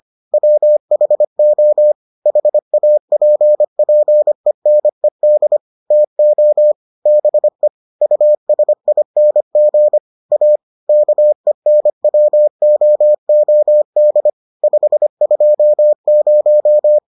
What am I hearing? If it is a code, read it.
Morse code, heard as a keyed tone, about 25 wpm, 600 Hz. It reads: WHO HAPPENED TO BE USING A KENWOOD 520